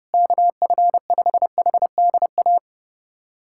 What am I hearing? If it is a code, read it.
Morse code, heard as a keyed tone, about 30 wpm, 700 Hz. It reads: KF5HDA
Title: USA Callsign DE Callsign 30wpm